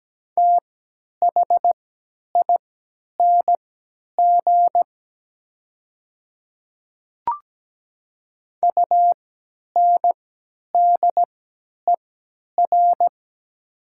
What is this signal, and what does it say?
Morse code, heard as a keyed tone, about 17 wpm, spaced as for 10 wpm, 690 Hz. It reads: THING UNDER
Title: 3x Character Spacing - Top 300 Words - 17wpm